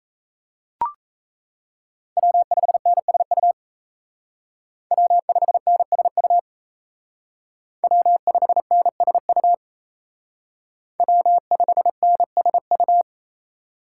Morse code, heard as a keyed tone, about 37 wpm, 700 Hz. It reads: W5NSU W5NSU W5NSU W5NSU